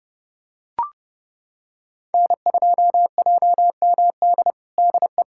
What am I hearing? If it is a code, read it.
Morse code, heard as a keyed tone, about 32 wpm, 700 Hz. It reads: N2JMD DE
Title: USA Callsign DE Callsign 30wpm